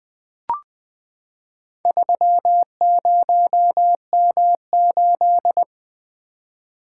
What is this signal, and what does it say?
Morse code, heard as a keyed tone, about 20 wpm, 700 Hz. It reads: 30M8
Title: Four Letter-Number Combinations - 20wpm